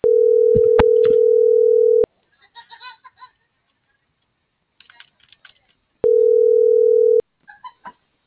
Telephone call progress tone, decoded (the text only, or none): ringback tone